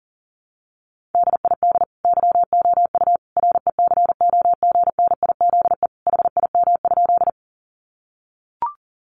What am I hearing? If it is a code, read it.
Morse code, heard as a keyed tone, about 40 wpm, 700 Hz. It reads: DID YOU RECOGNIZE HIM?
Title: Sentences from Top 1400 Words 40wpm